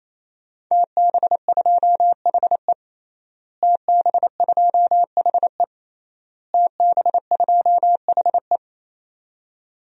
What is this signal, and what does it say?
Morse code, heard as a keyed tone, about 28 wpm, 700 Hz. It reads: TB2HE TB2HE TB2HE
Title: DXCC-100 Call Signs - 5 Characters - Repeated 3x - 28wpm